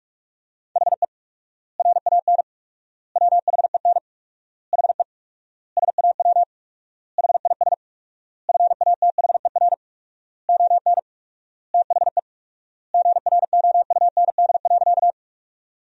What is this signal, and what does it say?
Morse code, heard as a keyed tone, about 42 wpm, 690 Hz. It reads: HE RAN WHEN HE SAW HIS FATHER ON THE GROUND.
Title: 4x Word-Spacing - Sentences from Top 500 Words - 45wpm